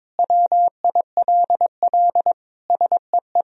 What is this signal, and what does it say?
Morse code, heard as a keyed tone, about 22 wpm, 700 Hz. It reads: WILL SEE
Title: Sentences from Top 1000 Words - Review Part 2 - 22wpm